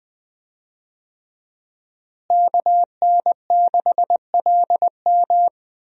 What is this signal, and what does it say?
Morse code, heard as a keyed tone, about 20 wpm, 700 Hz. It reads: KN6LM
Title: USA Call Signs - 5 Characters - Repeated 3x - 20wpm